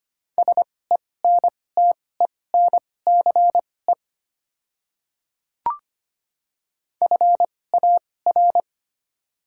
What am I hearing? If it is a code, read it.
Morse code, heard as a keyed tone, about 25 wpm, spaced as for 18 wpm, 700 Hz. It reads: SENTENCE FAR
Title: 2x Character Spacing - Top 500 Words - 25wpm